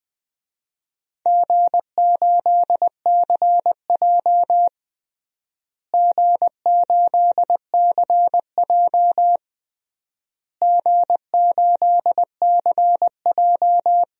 Morse code, heard as a keyed tone, about 20 wpm, 700 Hz. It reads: G8CJ G8CJ G8CJ
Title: DXCC-100 Call Signs - 4 Characters - Repeated 3x - 20wpm